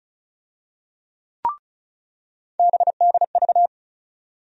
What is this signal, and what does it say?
Morse code, heard as a keyed tone, about 34 wpm, 700 Hz. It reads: BDV
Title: USA Call Sign Postfixes - 3 Letters 35wpm